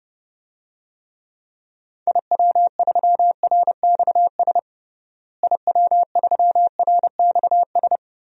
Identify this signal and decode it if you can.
Morse code, heard as a keyed tone, about 32 wpm, 700 Hz. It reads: IW3RXS IW3RXS
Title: DXCC-100 Call Signs - 6 Characters - Repeated 3x - 30wpm